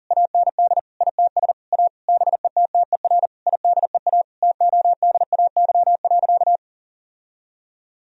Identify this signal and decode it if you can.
Morse code, heard as a keyed tone, about 40 wpm, 700 Hz. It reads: AND ITS A BETTER IDEA TODAY.
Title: Sentences from Top 1000 Words - Review Part 4 - 40wpm